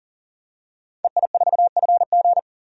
Morse code, heard as a keyed tone, about 40 wpm, 700 Hz. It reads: EI4FG
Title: DXCC-100 Call Signs - 5 Characters - 40wpm